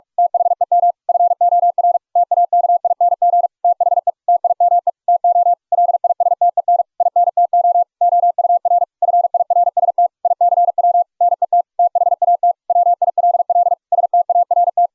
Morse code, heard as a keyed tone, about 41 wpm, 700 Hz. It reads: THEM FOR TAKING THE TIME TO LISTEN INTO OUR FIRST ICW NET THAT WILL START